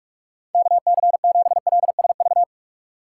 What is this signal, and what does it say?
Morse code, heard as a keyed tone, about 46 wpm, 700 Hz. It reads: K C 7 L S V
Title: USA Call Signs - 6 Characters - 6x Character Spacing - 45wpm